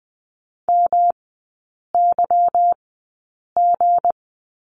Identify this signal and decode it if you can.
Morse code, heard as a keyed tone, about 20 wpm, spaced as for 8 wpm, 700 Hz. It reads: MYG
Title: 184 Lesson 30e - Letter b - Three Characters 20wpm